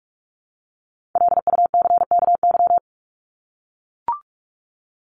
Morse code, heard as a keyed tone, about 44 wpm, 700 Hz. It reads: LUCKY
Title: Top 1751-2000 Words 45wpm